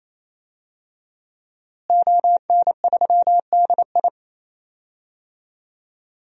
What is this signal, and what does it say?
Morse code, heard as a keyed tone, about 28 wpm, 700 Hz. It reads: ON3DI
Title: DXCC-100 Call Signs - 5 Characters - Repeated 3x - 28wpm